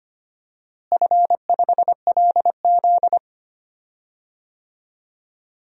Morse code, heard as a keyed tone, about 25 wpm, 700 Hz. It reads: F5LZ
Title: DXCC-100 Call Signs - 4 Characters - Repeated 2x - 25wpm